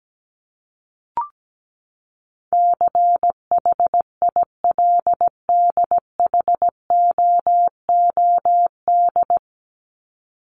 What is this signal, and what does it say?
Morse code, heard as a keyed tone, about 17 wpm, 700 Hz. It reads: CHILDHOOD
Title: Top 1751-2000 Words 17wpm